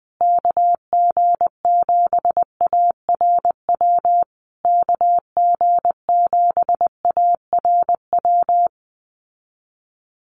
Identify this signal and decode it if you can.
Morse code, heard as a keyed tone, about 20 wpm, 700 Hz. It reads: KG7ARW KG7ARW
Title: USA Call Signs Repeated 3x 20wpm